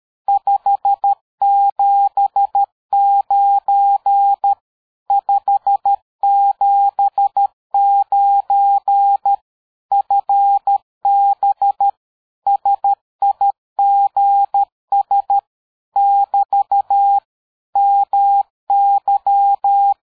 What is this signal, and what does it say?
Morse code, heard as a keyed tone, about 13 wpm, 800 Hz. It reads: 579 579 FB SIGS <BT> MY
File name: dk5ke / QSO 60BpM